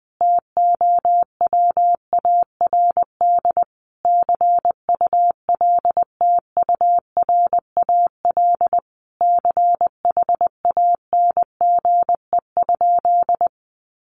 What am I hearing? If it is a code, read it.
Morse code, heard as a keyed tone, about 20 wpm, 700 Hz. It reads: TOWARD CULTURAL CHANGE?
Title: Sentences from Top 1500 Words 20wpm